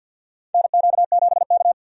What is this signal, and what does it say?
Morse code, heard as a keyed tone, about 52 wpm, 700 Hz. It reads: NQ7K
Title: USA Call Signs - 4 Characters - 50wpm